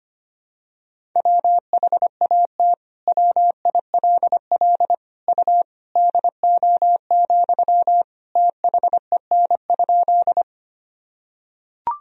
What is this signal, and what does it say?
Morse code, heard as a keyed tone, about 25 wpm, 700 Hz. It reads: WHAT WILL U DO, THEN?
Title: Sentences from Top 100 Words and Abbreviations 25wpm